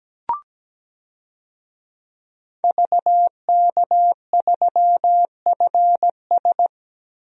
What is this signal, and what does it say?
Morse code, heard as a keyed tone, about 17 wpm, 700 Hz. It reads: VK3FS
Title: USA Call Signs - 5 Characters - Repeated 2x - 17wpm